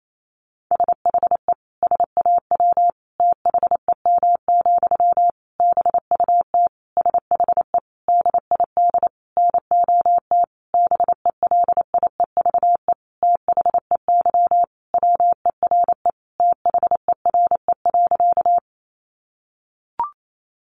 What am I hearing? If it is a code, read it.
Morse code, heard as a keyed tone, about 28 wpm, 700 Hz. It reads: SHE SAW THEM, BUT SHE DID NOT BELIEVE THEY WERE THERE.
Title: Sentences from Top 1000 Words 28wpm